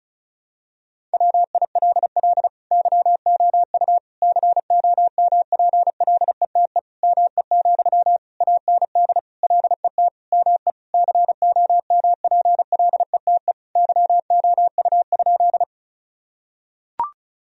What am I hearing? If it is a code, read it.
Morse code, heard as a keyed tone, about 36 wpm, 700 Hz. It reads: WILL YOU COMPLETE ME, AND LET ME COMPLETE YOU?
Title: Sentences from Top 1000 Words - Review Part 2 - 35wpm